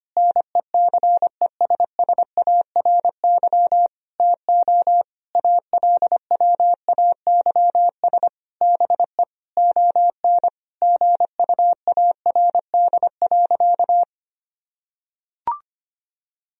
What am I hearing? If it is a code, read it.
Morse code, heard as a keyed tone, about 25 wpm, 700 Hz. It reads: NECESSARY TO ALWAYS BE ON GUARD.
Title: Sentences from Top 1000 Words - Review Part 4 - 25wpm